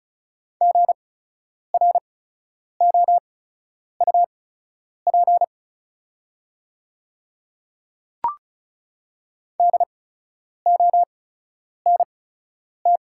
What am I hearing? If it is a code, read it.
Morse code, heard as a keyed tone, about 36 wpm, spaced as for 10 wpm, 690 Hz. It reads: GROUP DONT
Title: 8x Character Spacing - Top 300 Words - 35wpm